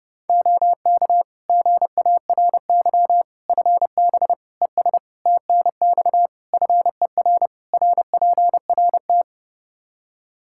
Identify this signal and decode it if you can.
Morse code, heard as a keyed tone, about 30 wpm, 700 Hz. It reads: OK GARY FB ES TNX FER RPRT
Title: Practice QSO - Exchange 2 - Info - 30wpm